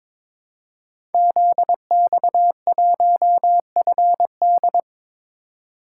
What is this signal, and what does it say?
Morse code, heard as a keyed tone, about 22 wpm, 700 Hz. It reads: ZX1FD
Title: Code Groups - 5 Letter:Numbers - 22wpm